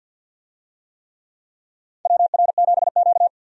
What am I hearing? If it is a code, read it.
Morse code, heard as a keyed tone, about 48 wpm, 700 Hz. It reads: WR7Q
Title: USA Call Signs - 4 Characters - 50wpm